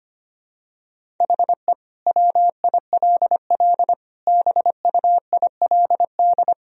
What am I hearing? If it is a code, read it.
Morse code, heard as a keyed tone, about 25 wpm, 700 Hz. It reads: HE WILL BUILD
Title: Sentences from Top 1000 Words - Review Part 1 - 25wpm